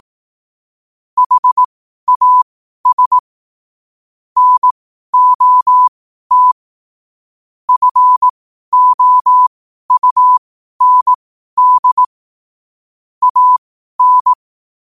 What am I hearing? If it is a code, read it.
Morse code, heard as a keyed tone, about 18 wpm, spaced as for 12 wpm, 1000 Hz. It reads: HAS NOT FOUND AN
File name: qotdfiles / 12wpm